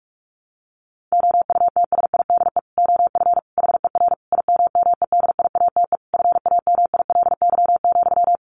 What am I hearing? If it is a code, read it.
Morse code, heard as a keyed tone, about 44 wpm, 700 Hz. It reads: OUTSIDE OF HER IMMEDIATE FAMILY,
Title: Sentences from Top 1200 Words 45wpm